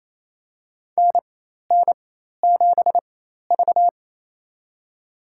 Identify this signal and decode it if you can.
Morse code, heard as a keyed tone, about 28 wpm, spaced as for 13 wpm, 700 Hz. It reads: NN7V
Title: USA Call Signs - 4x Character Spacing - 28wpm